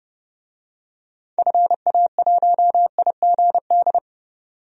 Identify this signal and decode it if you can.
Morse code, heard as a keyed tone, about 30 wpm, 690 Hz. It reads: FA1IGD
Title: DXCC-100 Call Signs - 6 Characters - 30wpm